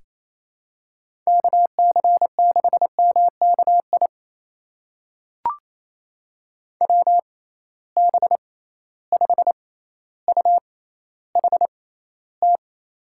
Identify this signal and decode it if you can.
Morse code, heard as a keyed tone, about 28 wpm, spaced as for 14 wpm, 700 Hz. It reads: KC6MKI WB5UHT
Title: USA Call Signs - 6 Characters - 6x Character Spacing - 28wpm